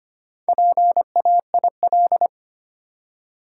Morse code, heard as a keyed tone, about 25 wpm, 690 Hz. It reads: PAIL
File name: practice / 25wpm / Easy Words with Hints - 25wpm